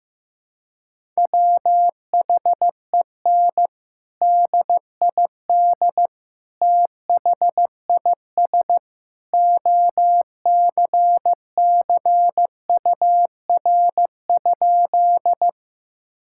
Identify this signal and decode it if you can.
Morse code, heard as a keyed tone, about 16 wpm, 700 Hz. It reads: WHEN DID THIS OCCUR?
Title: Sentences from Top 1000 Words - Review Part 6 - 15wpm